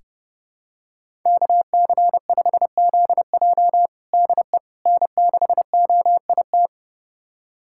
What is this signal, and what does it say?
Morse code, heard as a keyed tone, about 32 wpm, 700 Hz. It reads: KC5ZJ DE N6OIT
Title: USA Callsign DE Callsign - 5 Characters - 30wpm